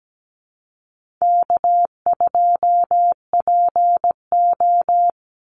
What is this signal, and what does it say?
Morse code, heard as a keyed tone, about 17 wpm, 700 Hz. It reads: K2PO
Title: Call Signs Repeated 2x - 17wpm